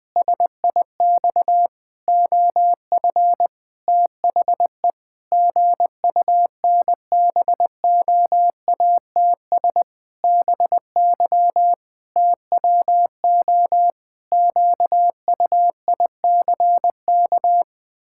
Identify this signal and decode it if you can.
Morse code, heard as a keyed tone, about 20 wpm, 700 Hz. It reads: SIX OF THE GUNBOATS BY TWO QUICK
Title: Pangram Challenge v2 - 20wpm